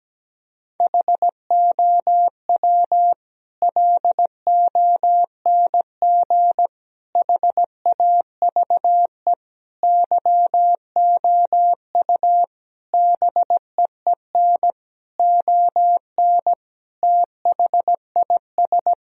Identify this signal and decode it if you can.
Morse code, heard as a keyed tone, about 17 wpm, 700 Hz. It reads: HOW LONG HAVE YOU BEEN ON THIS
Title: Sentences from Top 1000 Words - Review Part 3 - 17wpm